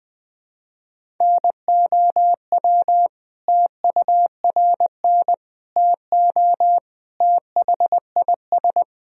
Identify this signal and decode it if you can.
Morse code, heard as a keyed tone, about 21 wpm, 700 Hz. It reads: NOW TURN TO THIS
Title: Sentences from Top 1000 Words - Review Part 1 - 20wpm